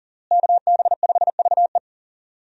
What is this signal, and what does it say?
Morse code, heard as a keyed tone, about 40 wpm, 700 Hz. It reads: KB5VE
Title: USA Call Signs - 5 Characters - Repeated 3x - 40wpm